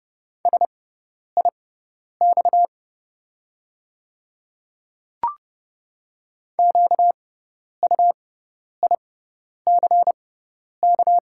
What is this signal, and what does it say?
Morse code, heard as a keyed tone, about 30 wpm, spaced as for 10 wpm, 700 Hz. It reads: SIX QUICK
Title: 6x Character Spacing - Top 500 Words - 30wpm